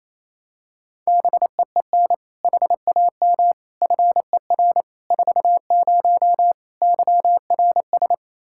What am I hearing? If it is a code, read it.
Morse code, heard as a keyed tone, about 27 wpm, 700 Hz. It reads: BEEN HAM FER 40 YRS
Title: Practice QSO - Exchange 3 - More Info - 28wpm